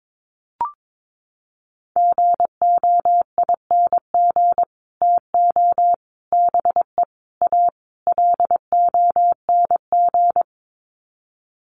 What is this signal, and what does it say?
Morse code, heard as a keyed tone, about 22 wpm, 700 Hz. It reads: GOING TO BE A LONG
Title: Sets of 5 Words Encore2 22wpm